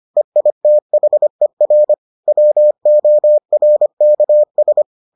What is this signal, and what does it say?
Morse code, heard as a keyed tone, about 25 wpm, 600 Hz. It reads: EITHER WORKS